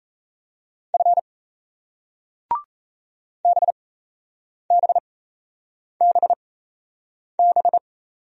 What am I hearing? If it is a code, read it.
Morse code, heard as a keyed tone, about 38 wpm, spaced as for 8 wpm, 700 Hz. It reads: F BBBB